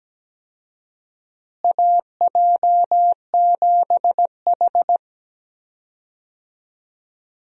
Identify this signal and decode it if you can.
Morse code, heard as a keyed tone, about 17 wpm, 700 Hz. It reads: AJ7H
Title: USA Call Signs - 4 Characters - Repeated 2x - 17wpm